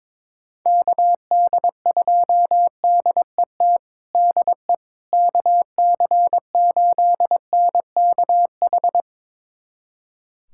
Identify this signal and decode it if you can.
Morse code, heard as a keyed tone, about 21 wpm, 700 Hz. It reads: KD2DET DE KC8NKH
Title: USA Callsign DE Callsign - 6 Characters - 22wpm